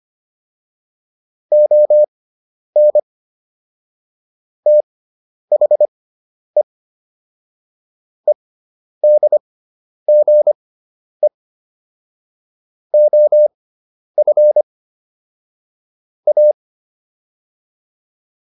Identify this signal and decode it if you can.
Morse code, heard as a keyed tone, about 25 wpm, spaced as for 10 wpm, 600 Hz. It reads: ON THE EDGE OF A